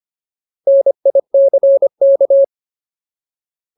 Morse code, heard as a keyed tone, about 25 wpm, 550 Hz. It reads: NICK
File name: Ultimate Word List / 25WPM Ultimate